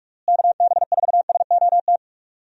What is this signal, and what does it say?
Morse code, heard as a keyed tone, about 46 wpm, 700 Hz. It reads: KB4SOT
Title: USA Call Signs - 6 Characters - Repeated 2x - 45wpm